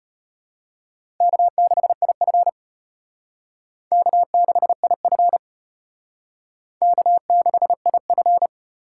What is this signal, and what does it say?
Morse code, heard as a keyed tone, about 32 wpm, 700 Hz. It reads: K6IF K6IF K6IF